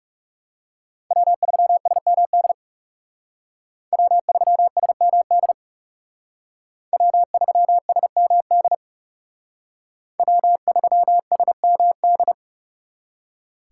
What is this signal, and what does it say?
Morse code, heard as a keyed tone, about 36 wpm, 700 Hz. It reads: W3SMD W3SMD W3SMD W3SMD
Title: Speed Racing - USA 1x3 Call Signs 30wpm